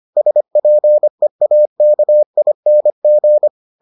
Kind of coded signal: Morse code

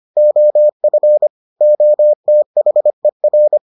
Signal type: Morse code

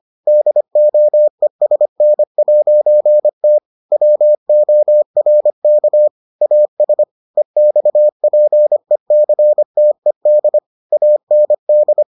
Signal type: Morse code

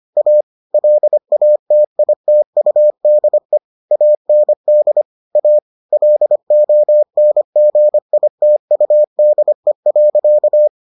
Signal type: Morse code

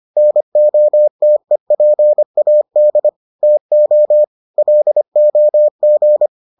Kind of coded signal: Morse code

